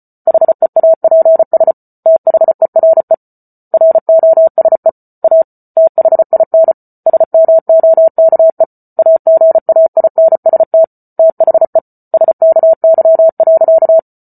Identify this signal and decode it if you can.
Morse code, heard as a keyed tone, about 34 wpm, 650 Hz. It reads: HEAPS THERE ROSE A THIN SMOKE AGAINST THE SKY.
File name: Book Two / 08-Chapter 8 - 34 WPM